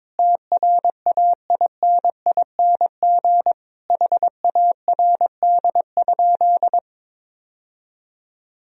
Morse code, heard as a keyed tone, about 22 wpm, 700 Hz. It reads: TRAINING HARD?
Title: Sentences from Top 1000 Words - Review Part 4 - 22wpm